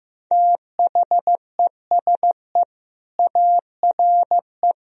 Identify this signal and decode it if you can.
Morse code, heard as a keyed tone, about 15 wpm, 700 Hz. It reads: THESE ARE
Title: Sentences from Top 1000 Words - Review Part 3 - 15wpm